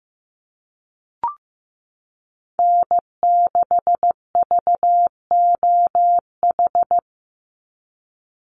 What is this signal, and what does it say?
Morse code, heard as a keyed tone, about 15 wpm, 700 Hz. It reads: N6VOH
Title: Call Signs Repeated 3x - 15wpm